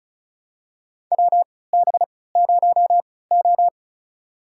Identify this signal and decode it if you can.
Morse code, heard as a keyed tone, about 34 wpm, spaced as for 20 wpm, 700 Hz. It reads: WB0O
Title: USA Call Signs - 4 Characters - 3x Character Spacing - 35wpm